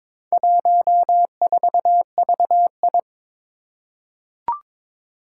Morse code, heard as a keyed tone, about 22 wpm, 700 Hz. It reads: 14VI
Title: Four Challenging Letter-Number Combinations 22wpm